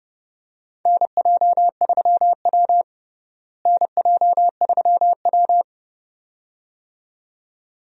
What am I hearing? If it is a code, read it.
Morse code, heard as a keyed tone, about 30 wpm, 700 Hz. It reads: NJ3W NJ3W
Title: USA Call Signs - 4 Characters - Repeated 2x - 30wpm